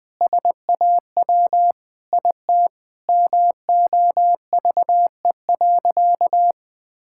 Morse code, heard as a keyed tone, about 21 wpm, 700 Hz. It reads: SAW IT MOVE.
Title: Sentences with 8-Word Phrases - 20wpm